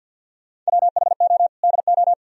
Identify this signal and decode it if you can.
Morse code, heard as a keyed tone, about 50 wpm, 700 Hz. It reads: WHO DO
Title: Sentences from Top 1000 Words - Review Part 4 - 50wpm